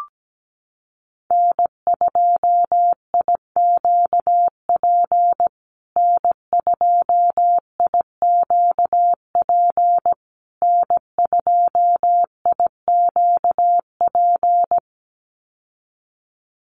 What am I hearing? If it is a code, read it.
Morse code, heard as a keyed tone, about 17 wpm, 700 Hz. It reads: N2IQP N2IQP N2IQP
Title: USA Call Signs Repeated 3x 17wpm